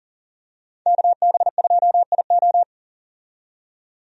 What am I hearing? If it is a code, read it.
Morse code, heard as a keyed tone, about 40 wpm, 700 Hz. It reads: KB2IO
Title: USA Call Signs - 5 Characters - 40wpm